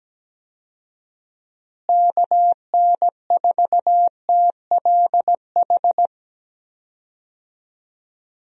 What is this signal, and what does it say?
Morse code, heard as a keyed tone, about 17 wpm, 700 Hz. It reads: KN4TLH
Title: USA Call Signs - 6 Characters - Repeated 3x - 17wpm